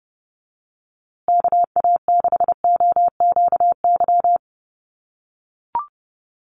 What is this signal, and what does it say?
Morse code, heard as a keyed tone, about 30 wpm, 700 Hz. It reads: KA6OQY
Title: USA Call Signs Repeated 2x 30wpm